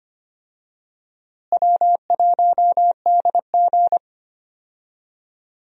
Morse code, heard as a keyed tone, about 25 wpm, 700 Hz. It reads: W1DG
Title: USA Call Signs - 4 Characters - Repeated 3x - 25wpm